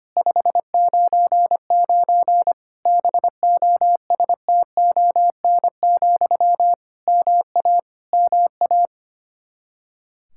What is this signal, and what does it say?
Morse code, heard as a keyed tone, about 25 wpm, 700 Hz. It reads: 599 BOSTON, MA MA
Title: Practice QSO - Exchange 1 - Beginning - 25wpm